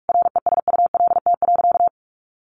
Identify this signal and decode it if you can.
Morse code, heard as a keyed tone, about 44 wpm, 700 Hz. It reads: RESULT.
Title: Sentences from Top 800 Words 45wpm